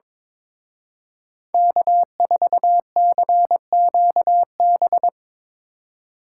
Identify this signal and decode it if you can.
Morse code, heard as a keyed tone, about 24 wpm, 700 Hz. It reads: K4CQB
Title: Speed Racing - USA 1x3 Call Signs 22wpm